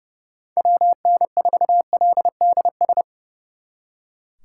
Morse code, heard as a keyed tone, about 37 wpm, 700 Hz. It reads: WN4LDS
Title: Speed Racing - USA Call Signs 30wpm